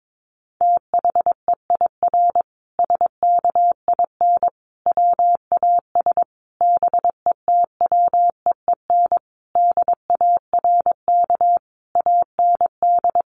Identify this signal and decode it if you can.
Morse code, heard as a keyed tone, about 22 wpm, 700 Hz. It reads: THEIR SKIN WAS BETWEEN DARK AND
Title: Sentences from Top 1300 Words 22wpm